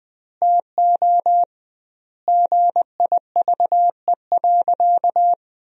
Morse code, heard as a keyed tone, about 20 wpm, 700 Hz. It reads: TO GIVE.
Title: 2x Word-Spacing - Sentences from Top 200 Words - 20wpm